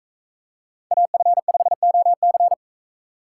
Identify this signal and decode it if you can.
Morse code, heard as a keyed tone, about 40 wpm, 700 Hz. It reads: AF5OC